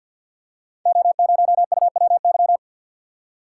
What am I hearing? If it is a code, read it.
Morse code, heard as a keyed tone, about 51 wpm, 700 Hz. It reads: O0UWY
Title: Code Groups - 5 Letter:Numbers - 50wpm